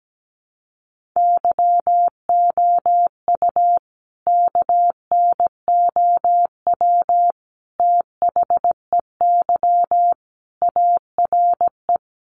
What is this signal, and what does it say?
Morse code, heard as a keyed tone, about 17 wpm, 700 Hz. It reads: YOU KNOW THEY ARE
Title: Sentences from Top 1400 Words 17wpm